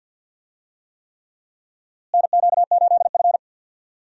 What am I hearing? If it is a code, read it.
Morse code, heard as a keyed tone, about 49 wpm, 700 Hz. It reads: NQ8F
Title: USA Call Signs - 4 Characters - Repeated 3x - 50wpm